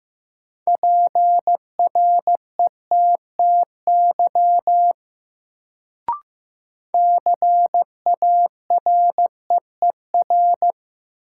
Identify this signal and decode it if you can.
Morse code, heard as a keyed tone, about 15 wpm, 700 Hz. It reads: PRETTY CAREER
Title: Sentences from Top 1000 Words - New Word Drill - 15wpm